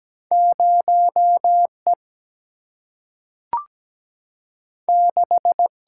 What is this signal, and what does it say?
Morse code, heard as a keyed tone, about 17 wpm, 700 Hz. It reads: 0E 6
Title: Single-Letter Single-Number 17wpm